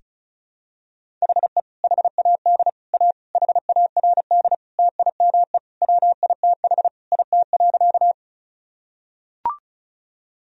Morse code, heard as a keyed tone, about 36 wpm, 700 Hz. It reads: HE HAD A HARD TIME WITH IT.
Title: Sentences from Top 1000 Words - Review Part 1 - 35wpm